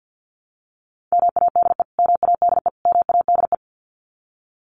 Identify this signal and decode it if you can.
Morse code, heard as a keyed tone, about 48 wpm, 700 Hz. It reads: MADE MADE MADE
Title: Top 200 Words Repeated 50wpm